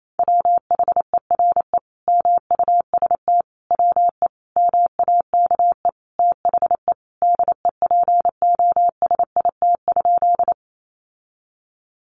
Morse code, heard as a keyed tone, about 28 wpm, 700 Hz. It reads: WHERE MUST WE MAKE THE DEPOSIT?
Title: Sentences from Top 1300 Words 28wpm